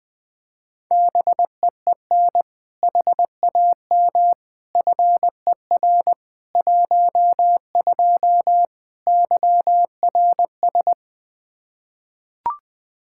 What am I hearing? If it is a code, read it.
Morse code, heard as a keyed tone, about 20 wpm, 700 Hz. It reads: BEEN HAM FER 12 YRS
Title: Practice QSO - Exchange 3 - More Info - 20wpm